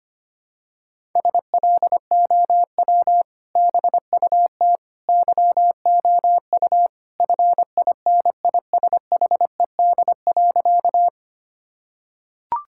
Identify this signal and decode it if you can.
Morse code, heard as a keyed tone, about 25 wpm, 700 Hz. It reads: SLOW BUT YOU FINISHED.
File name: practice / 25wpm / Sentences with 2-Word Phrases - 25wpm